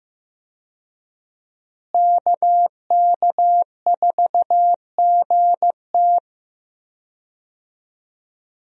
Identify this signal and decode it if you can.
Morse code, heard as a keyed tone, about 15 wpm, 700 Hz. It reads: KK4GT
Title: USA Call Signs - 5 Characters - Repeated 2x - 15wpm